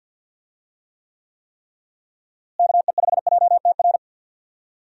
Morse code, heard as a keyed tone, about 51 wpm, 700 Hz. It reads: KE5JTR
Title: USA Call Signs - 6 Characters - Repeated 3x - 50wpm